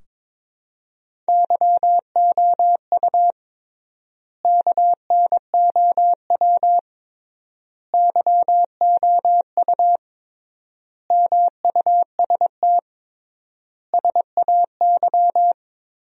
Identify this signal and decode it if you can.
Morse code, heard as a keyed tone, about 22 wpm, 700 Hz. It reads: YOU KNOW YOU MUST SAY
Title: 3x Word-Spacing - Sentences from Top 500 Words - 22wpm